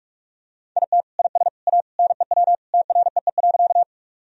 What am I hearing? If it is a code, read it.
Morse code, heard as a keyed tone, about 46 wpm, 700 Hz. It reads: IT IS A NEW TREE.
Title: Sentences from Top 1000 Words - Review Part 1 - 45wpm